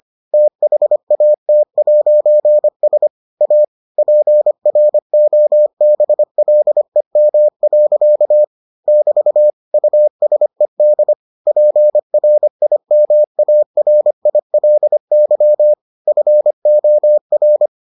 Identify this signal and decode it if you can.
Morse code, heard as a keyed tone, about 25 wpm, 600 Hz. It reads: THAT'S A PROBLEM. <BT> USED PRIMARILY FOR